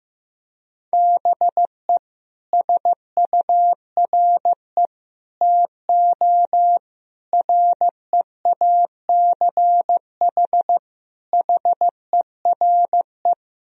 Morse code, heard as a keyed tone, about 15 wpm, 700 Hz. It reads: BE SURE TO REACH HERE
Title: Sentences from Top 1000 Words - Review Part 2 - 15wpm